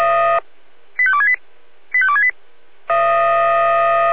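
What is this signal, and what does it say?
Five-tone selective call (ZVEI), heard as quick runs of five tones, tones of about 70 ms. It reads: 85268 85268
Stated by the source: Five-tone ZVEI selective call received by SDR.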